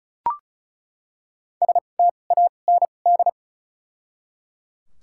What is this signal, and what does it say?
Morse code, heard as a keyed tone, about 34 wpm, spaced as for 25 wpm, 700 Hz. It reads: STAND
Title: 2x Character Spacing - Top 500 Words - 35wpm